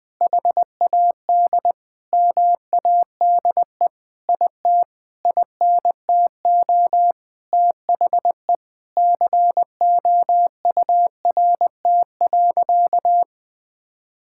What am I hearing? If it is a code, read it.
Morse code, heard as a keyed tone, about 21 wpm, 700 Hz. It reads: HAD MADE IT INTO THE COURT.
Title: Sentences from Top 1600 Words 20wpm